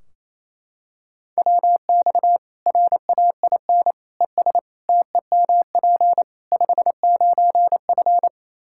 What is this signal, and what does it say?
Morse code, heard as a keyed tone, about 27 wpm, 700 Hz. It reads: WX RAIN ES TEMP 59F
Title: Practice QSO - Exchange 2 - Info - 28wpm